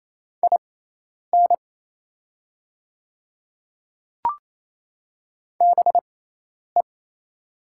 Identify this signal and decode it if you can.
Morse code, heard as a keyed tone, about 28 wpm, spaced as for 10 wpm, 700 Hz. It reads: IN BE
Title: 6x Character Spacing - Top 500 Words - 28wpm